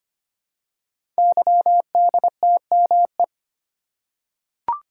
Morse code, heard as a keyed tone, about 25 wpm, 700 Hz. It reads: YDTME
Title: Code Groups - 5 Letters - 25wpm